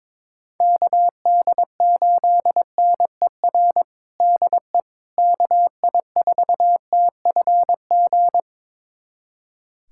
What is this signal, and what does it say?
Morse code, heard as a keyed tone, about 22 wpm, 700 Hz. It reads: KD8NER DE KI4TFG
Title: USA Callsign DE Callsign - 6 Characters - 22wpm